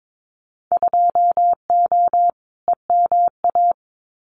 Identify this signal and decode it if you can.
Morse code, heard as a keyed tone, about 22 wpm, 700 Hz. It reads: 2O EMA
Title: Winter Field Day Exchanges Encore 22wpm